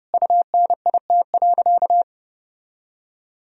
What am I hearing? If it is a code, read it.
Morse code, heard as a keyed tone, about 30 wpm, 700 Hz. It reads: UNIT.
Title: Sentences from Top 1000 Words - Review Part 2 - 30wpm